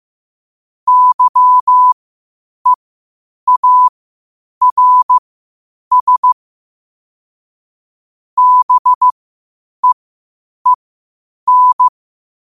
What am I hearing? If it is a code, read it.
Morse code, heard as a keyed tone, about 15 wpm, spaced as for 8 wpm, 1000 Hz. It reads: YEARS BEEN
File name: qotdfiles / 8wpm